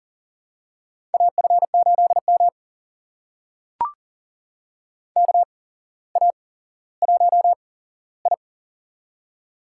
Morse code, heard as a keyed tone, about 38 wpm, spaced as for 17 wpm, 700 Hz. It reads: AF8M KA1I
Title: USA Call Signs - 4 Characters - 8x Character Spacing - 40wpm